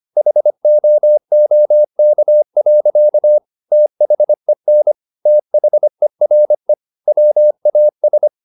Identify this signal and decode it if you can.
Morse code, heard as a keyed tone, about 25 wpm, 600 Hz. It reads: HOOK. THEN THERE WAS
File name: the old man and the sea part 1 25wpm